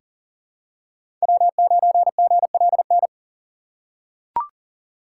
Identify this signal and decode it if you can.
Morse code, heard as a keyed tone, about 40 wpm, 700 Hz. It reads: W9GLN
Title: USA Call Signs - 5 Characters - 2x Character Spacing - 40wpm